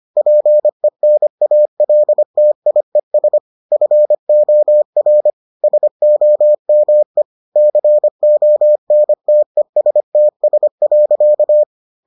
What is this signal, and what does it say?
Morse code, heard as a keyed tone, about 25 wpm, 600 Hz. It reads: PENALTIES FOR SOME CONTESTS.